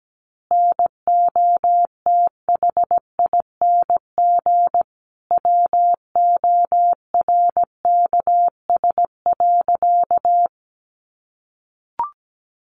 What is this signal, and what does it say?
Morse code, heard as a keyed tone, about 17 wpm, 700 Hz. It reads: NOTHING WORKS.
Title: Additional Easy Phrases 17wpm